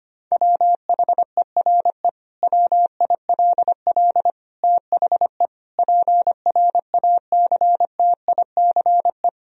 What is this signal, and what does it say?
Morse code, heard as a keyed tone, about 25 wpm, 700 Hz. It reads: WHERE WILL THE PRACTICE
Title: Sentences from Top 1000 Words - Review Part 4 - 25wpm